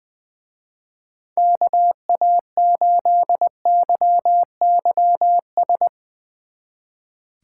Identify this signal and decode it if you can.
Morse code, heard as a keyed tone, about 22 wpm, 700 Hz. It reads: KA8YYS
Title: Speed Racing - USA Call Signs 20wpm